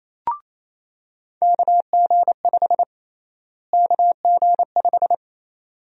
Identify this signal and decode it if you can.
Morse code, heard as a keyed tone, about 28 wpm, 700 Hz. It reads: KG5 KG5
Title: USA Call Sign Prefixes - Repeated 2x - 28wpm